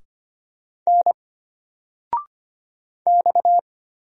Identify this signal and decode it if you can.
Morse code, heard as a keyed tone, about 25 wpm, 700 Hz. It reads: N X
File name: Single-Letter-Number-25wpm